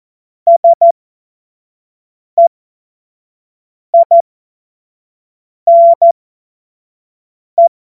Morse code, heard as a keyed tone, about 14 wpm, spaced as for 5 wpm, 680 Hz. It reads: SEINE
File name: Morsekurs / Witze / Witz 14.November 2023 25 F70